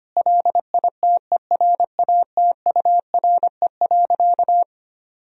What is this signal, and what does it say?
Morse code, heard as a keyed tone, about 25 wpm, 700 Hz. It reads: LITERATURE.
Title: Sentences from Top 1000 Words - Review Part 3 - 25wpm